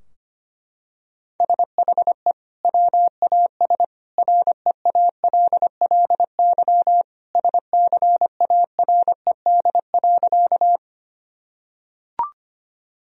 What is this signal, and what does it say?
Morse code, heard as a keyed tone, about 25 wpm, 700 Hz. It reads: SHE WAS REALLY SCARED.
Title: Sentences from Top 1000 Words - Review Part 4 - 25wpm